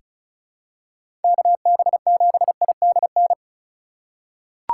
Morse code, heard as a keyed tone, about 36 wpm, 700 Hz. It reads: KB7IDN
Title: USA Call Signs - 6 Characters - 4x Character Spacing - 35wpm